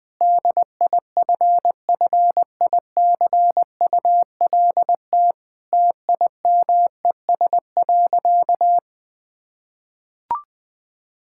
Morse code, heard as a keyed tone, about 21 wpm, 700 Hz. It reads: DIFFICULT TIMES.
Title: Difficult Sentences with Hints - 20wpm